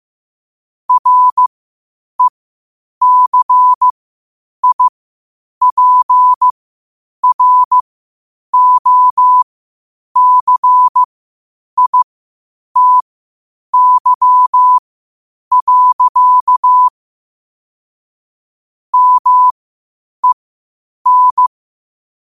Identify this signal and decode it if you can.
Morse code, heard as a keyed tone, about 15 wpm, spaced as for 8 wpm, 1000 Hz. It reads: RECIPROCITY. MEN
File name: 8wpm